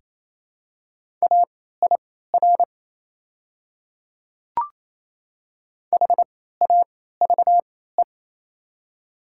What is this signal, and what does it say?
Morse code, heard as a keyed tone, about 28 wpm, spaced as for 16 wpm, 700 Hz. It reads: AIR HAVE
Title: 3x Character Spacing - Top 500 Words - 28wpm